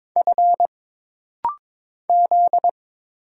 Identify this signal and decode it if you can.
Morse code, heard as a keyed tone, about 22 wpm, 700 Hz. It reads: F Z
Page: This rapid-fire timing is geared to push your ICR (Instant Character Recognition) skills.